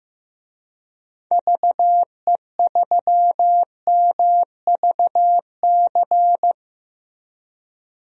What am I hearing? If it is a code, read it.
Morse code, heard as a keyed tone, about 15 wpm, 700 Hz. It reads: VE3MVC
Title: DXCC-100 Call Signs - 6 Characters - Repeated 3x - 15wpm